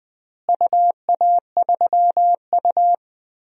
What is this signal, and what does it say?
Morse code, heard as a keyed tone, about 21 wpm, 700 Hz. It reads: UA3U
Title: DXCC-100 Call Signs - 4 Characters - Repeated 2x - 20wpm